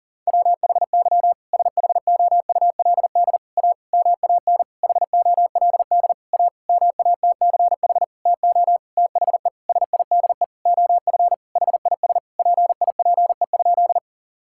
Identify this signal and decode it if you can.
Morse code, heard as a keyed tone, about 37 wpm, 700 Hz. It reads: WHY SHOULD A MAN HOLD A MATCH TO THE SIDE OF HIS PIPE?
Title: Sentences from Top 1600 Words 40wpm